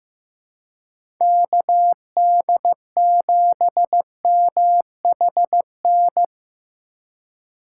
Morse code, heard as a keyed tone, about 15 wpm, 700 Hz. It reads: KD7MHN
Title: USA Call Signs - 6 Characters - 4x Character Spacing - 15wpm